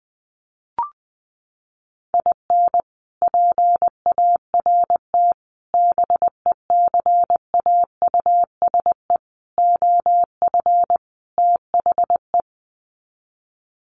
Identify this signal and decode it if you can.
Morse code, heard as a keyed tone, about 20 wpm, 700 Hz. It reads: IN PART BECAUSE OF THE
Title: Sets of 5 Words Encore2 20wpm